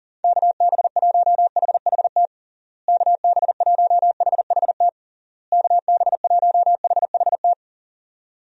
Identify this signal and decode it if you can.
Morse code, heard as a keyed tone, about 40 wpm, 700 Hz. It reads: KB1HHT KB1HHT KB1HHT
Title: USA Call Signs - 6 Characters - Repeated 3x - 40wpm